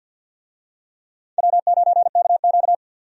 Morse code, heard as a keyed tone, about 51 wpm, 700 Hz. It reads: W9KQ
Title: USA Call Signs - 4 Characters - 50wpm